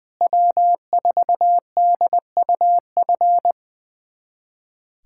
Morse code, heard as a keyed tone, about 20 wpm, 700 Hz. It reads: W4DUF
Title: USA Callsign DE Callsign - 5 Characters - 20wpm